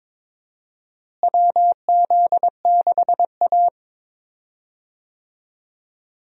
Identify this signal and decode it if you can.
Morse code, heard as a keyed tone, about 22 wpm, 700 Hz. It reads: WZ6A
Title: USA Call Signs - 4 Characters - Repeated 2x - 22wpm